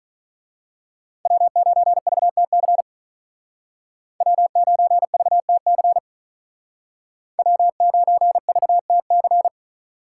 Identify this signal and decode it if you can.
Morse code, heard as a keyed tone, about 41 wpm, 700 Hz. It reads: W9VTC W9VTC W9VTC
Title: Speed Racing - USA 1x3 Call Signs 35wpm